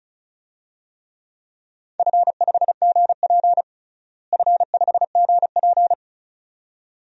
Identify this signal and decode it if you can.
Morse code, heard as a keyed tone, about 36 wpm, 700 Hz. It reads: F5GP F5GP
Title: DXCC-100 Call Signs - 4 Characters - Repeated 2x - 35wpm